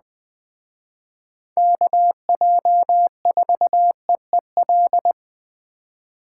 Morse code, heard as a keyed tone, about 20 wpm, 700 Hz. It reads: KJ4EEL
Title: Speed Racing - USA 2x3 Call Signs 17wpm